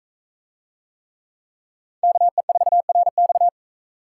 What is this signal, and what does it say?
Morse code, heard as a keyed tone, about 43 wpm, 700 Hz. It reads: KE4RX